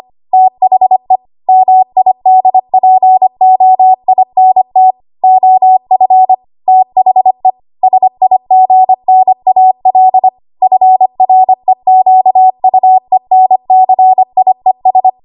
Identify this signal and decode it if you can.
Morse code, heard as a keyed tone, about 25 wpm, 750 Hz. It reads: THE MIDPOINT OF THE SIGNAL FREQUENCIES
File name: Archive / 25 WPM / 25WPM